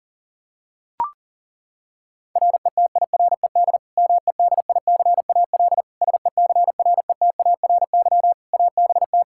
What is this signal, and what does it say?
Morse code, heard as a keyed tone, about 40 wpm, 700 Hz. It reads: RETIRED MEDICAL SECRETARY ABT
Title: K1USN SST QSO Practice 40wpm